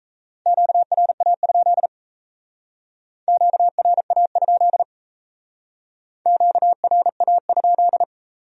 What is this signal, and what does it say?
Morse code, heard as a keyed tone, about 37 wpm, 700 Hz. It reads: QRA? QRA? QRA?